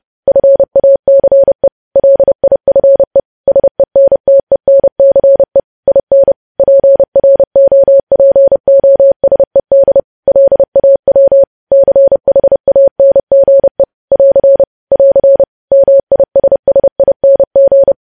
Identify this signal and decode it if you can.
Morse code, heard as a keyed tone, about 30 wpm, 550 Hz. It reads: FACE LIFE SENTENCE IN PROPOSED LAW CHANGE <AR> <AR> MISSING
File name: podcast / News headlines-1x-30x30